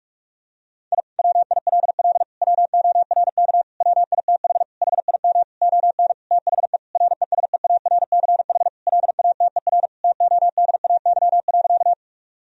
Morse code, heard as a keyed tone, about 46 wpm, 700 Hz. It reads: I WILL WORK WITH HIM ON THE RESEARCH LATER TODAY.
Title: Sentences from Top 1000 Words - Review Part 4 - 45wpm